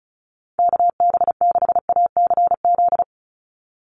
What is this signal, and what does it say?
Morse code, heard as a keyed tone, about 34 wpm, 700 Hz. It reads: KB6ACZ
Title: USA Call Signs 35wpm